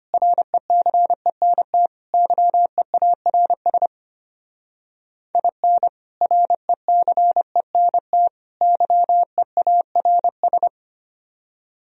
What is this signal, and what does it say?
Morse code, heard as a keyed tone, about 32 wpm, 700 Hz. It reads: RECENT YEARS IN RECENT YEARS